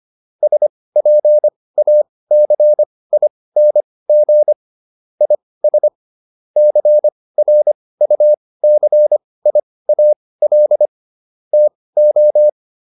Morse code, heard as a keyed tone, about 25 wpm, spaced as for 18 wpm, 600 Hz. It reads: SPACING IS CRUCIAL TO